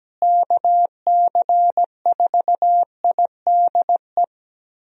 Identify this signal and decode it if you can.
Morse code, heard as a keyed tone, about 17 wpm, 700 Hz. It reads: KC4IDE
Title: USA Call Signs - 6 Characters - Repeated 2x - 17wpm